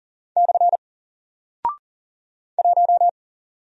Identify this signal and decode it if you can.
Morse code, heard as a keyed tone, about 40 wpm, 700 Hz. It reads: / 1
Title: Single Character (Letter:Number:Punctuation:Prosign) Alt 40wpm